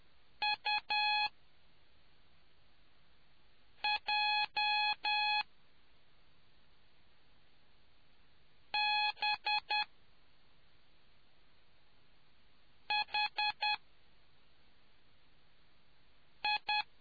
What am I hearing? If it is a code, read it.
Morse code, heard as a keyed tone, about 10 wpm, spaced as for 3 wpm, 840 Hz. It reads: UJBHI